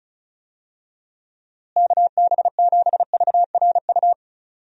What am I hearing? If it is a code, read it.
Morse code, heard as a keyed tone, about 36 wpm, 700 Hz. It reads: KB7VRU
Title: USA Call Signs - 6 Characters - Repeated 2x - 35wpm